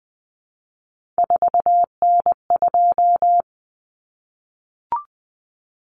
Morse code, heard as a keyed tone, about 20 wpm, 700 Hz. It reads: 4N2
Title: Common Cut Numbers - 3-Digit 20wpm